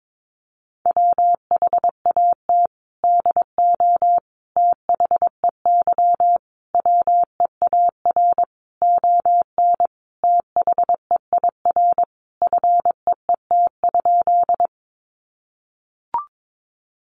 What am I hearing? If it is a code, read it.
Morse code, heard as a keyed tone, about 22 wpm, 700 Hz. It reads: WHAT DO THEY WEAR ON THEIR FEET?
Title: Sentences from Top 1500 Words 22wpm